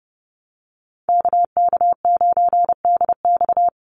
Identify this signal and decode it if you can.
Morse code, heard as a keyed tone, about 30 wpm, 700 Hz. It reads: KK9DX
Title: Call Signs Repeated 2x - 30wpm